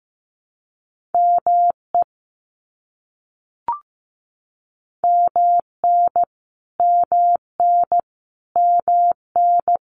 Morse code, heard as a keyed tone, about 15 wpm, 700 Hz. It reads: ME MN MN MN
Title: US States Repeated 3x 15wpm